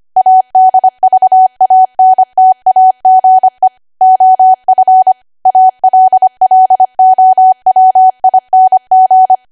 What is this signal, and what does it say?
Morse code, heard as a keyed tone, about 25 wpm, 750 Hz. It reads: ADVANTAGE OF ALLOWING
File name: Archive / 25 WPM / 25WPM